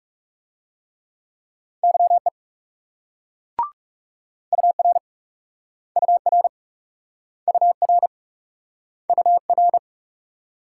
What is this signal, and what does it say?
Morse code, heard as a keyed tone, about 38 wpm, 700 Hz. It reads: YE UR UR UR UR